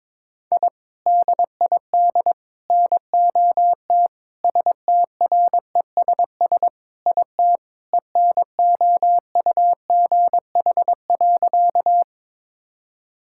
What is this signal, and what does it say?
Morse code, heard as a keyed tone, about 22 wpm, 700 Hz. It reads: I DID NOT STRESS IT ENOUGH.
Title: Sentences from Top 1000 Words - Review Part 6 - 22wpm